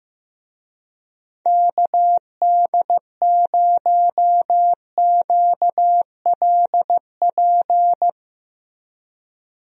Morse code, heard as a keyed tone, about 15 wpm, 700 Hz. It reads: KD0QLP
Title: USA Call Signs - 6 Characters - Repeated 3x - 15wpm